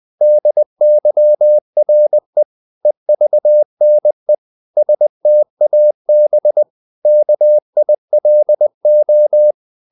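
Morse code, heard as a keyed tone, about 20 wpm, 600 Hz. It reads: DYRE EVNE STAB KILO
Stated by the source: Morse code practice of short Danish words at 20 wpm, from CW med Gnister.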